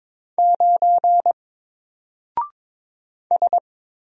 Morse code, heard as a keyed tone, about 22 wpm, 700 Hz. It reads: 9 S
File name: Single-Letter-Number-22wpm